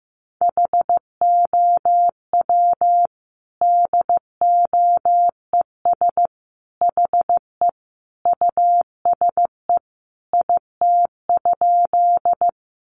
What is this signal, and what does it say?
Morse code, heard as a keyed tone, about 15 wpm, 700 Hz. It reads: HOW DOES HE USE IT?
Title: Sentences from Top 200 Words 15wpm